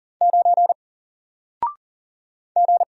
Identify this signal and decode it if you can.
Morse code, heard as a keyed tone, about 40 wpm, 700 Hz. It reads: 9 G
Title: Single Letter-Number Alt 40wpm